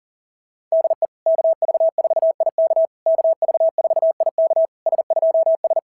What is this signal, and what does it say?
Morse code, heard as a keyed tone, about 40 wpm, 660 Hz. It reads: DE KV4IK KV4IK S2S
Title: SOTA QSO Practice 40wpm